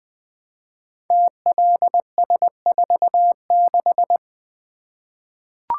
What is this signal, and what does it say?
Morse code, heard as a keyed tone, about 20 wpm, 700 Hz. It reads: TLS46
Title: Code Groups - 5 Letter:Numbers - 20wpm